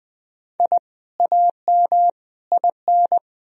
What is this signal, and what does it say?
Morse code, heard as a keyed tone, about 20 wpm, 700 Hz. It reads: I AM IN
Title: Sentences from Top 1000 Words - Review Part 4 - 20wpm